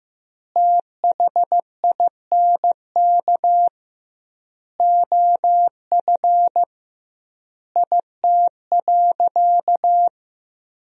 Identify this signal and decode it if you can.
Morse code, heard as a keyed tone, about 15 wpm, 700 Hz. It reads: THINK OF IT.
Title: 2x Word-Spacing - Sentences from Top 200 Words - 15wpm